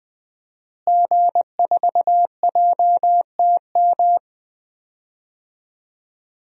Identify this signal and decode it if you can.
Morse code, heard as a keyed tone, about 20 wpm, 700 Hz. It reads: G4JTM
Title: DXCC-100 Call Signs - 5 Characters - Repeated 3x - 20wpm